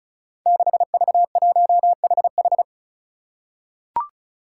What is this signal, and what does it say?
Morse code, heard as a keyed tone, about 34 wpm, 700 Hz. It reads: 6V1HH
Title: Five Challenging Letter-Number Combinations 35wpm